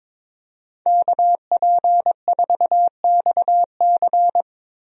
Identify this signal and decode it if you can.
Morse code, heard as a keyed tone, about 22 wpm, 700 Hz. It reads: KP4XC
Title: DXCC-100 Call Signs - 5 Characters - Repeated 3x - 22wpm